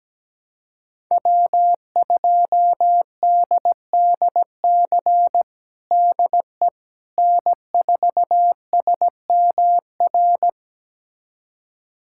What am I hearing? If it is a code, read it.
Morse code, heard as a keyed tone, about 17 wpm, 700 Hz. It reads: W2DDC DE N4SMR
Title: USA Callsign DE Callsign - 5 Characters - 17wpm